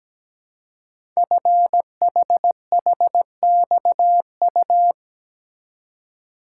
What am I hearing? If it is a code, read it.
Morse code, heard as a keyed tone, about 17 wpm, 700 Hz. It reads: FHHXU
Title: Code Groups - 5 Letters - 17wpm